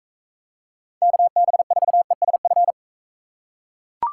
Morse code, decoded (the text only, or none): KB4ESF